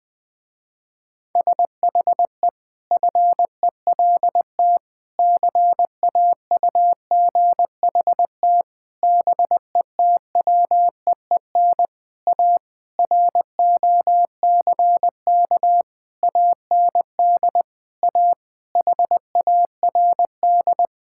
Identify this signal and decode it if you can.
Morse code, heard as a keyed tone, about 20 wpm, 700 Hz. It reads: SHE FELT CAUGHT BETWEEN A ROCK AND A HARD
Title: Sentences with 8-Word Phrases - Spoken Beginning - 20wpm